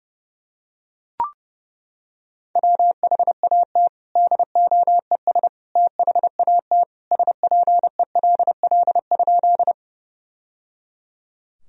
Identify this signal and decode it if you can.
Morse code, heard as a keyed tone, about 30 wpm, 700 Hz. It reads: WHAT DOES THAT SPELL?
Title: Sentences from Top 1000 Words - Review Part 1 - 30wpm